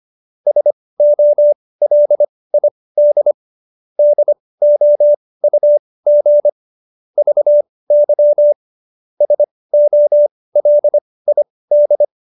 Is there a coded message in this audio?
Yes: Morse code from 0.5 to 12.0 s, heard as a keyed tone, about 25 wpm, spaced as for 18 wpm, 590 Hz. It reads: SOLID DOUG VY SOLID